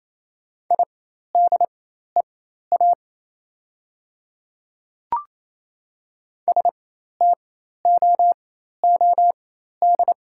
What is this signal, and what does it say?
Morse code, heard as a keyed tone, about 28 wpm, spaced as for 13 wpm, 700 Hz. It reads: IDEA STOOD
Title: 4x Character Spacing - Top 500 Words - 28wpm